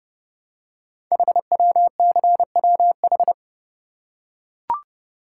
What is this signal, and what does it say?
Morse code, heard as a keyed tone, about 31 wpm, 700 Hz. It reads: HWCWH
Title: Code Groups - 5 Letters - 30wpm